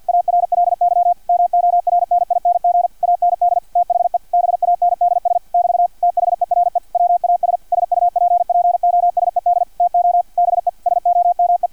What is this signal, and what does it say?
Morse code, heard as a keyed tone, about 50 wpm, 700 Hz, with radio noise. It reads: ARLY MORNING AND THE BANDS <BT> THERE WAS SUPPOSED TO BE SOME
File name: archived / CRSnet